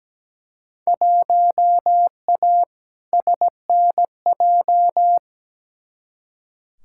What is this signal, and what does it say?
Morse code, heard as a keyed tone, about 17 wpm, 700 Hz. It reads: 1A SNJ
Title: Field Day Exchanges - 17wpm